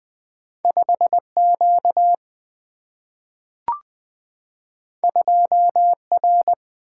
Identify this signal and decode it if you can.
Morse code, heard as a keyed tone, about 20 wpm, 700 Hz. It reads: 5 Q 2 R
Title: Single-Letter Single-Number 20wpm